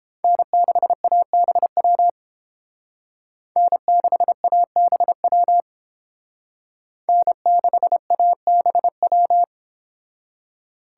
Morse code, heard as a keyed tone, about 28 wpm, 700 Hz. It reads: N6ABW N6ABW N6ABW